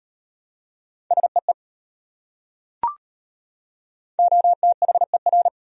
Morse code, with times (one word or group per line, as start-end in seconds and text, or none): 1.1-1.5 SEE
4.2-5.5 OTHER